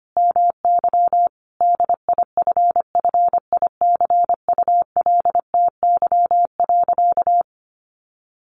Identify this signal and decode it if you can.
Morse code, heard as a keyed tone, about 25 wpm, 700 Hz. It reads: MY DIFFICULTY.
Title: Sentences from Top 1500 Words 25wpm